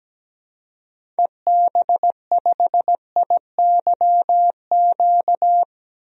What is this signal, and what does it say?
Morse code, heard as a keyed tone, about 17 wpm, 700 Hz. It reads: EB5IYQ
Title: DXCC-100 Call Signs - 6 Characters - Repeated 2x - 17wpm